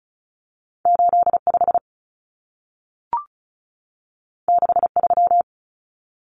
Morse code, heard as a keyed tone, about 36 wpm, 700 Hz. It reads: 85 63
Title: Common Cut Numbers - 2-Digit 35wpm